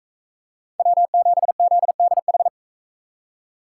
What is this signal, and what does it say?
Morse code, heard as a keyed tone, about 43 wpm, 700 Hz. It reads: W7ZDH